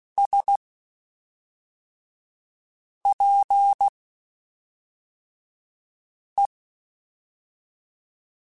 Morse code, heard as a keyed tone, about 16 wpm, spaced as for 3 wpm, 800 Hz. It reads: SPE
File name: Spaced Characters / Spaced Characters-mp3s / Weather5 1 sp.@5